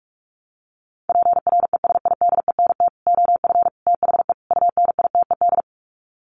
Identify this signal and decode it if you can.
Morse code, heard as a keyed tone, about 44 wpm, 700 Hz. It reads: PRESIDENT OF THE UNITED
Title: Sets of 4 Words 45wpm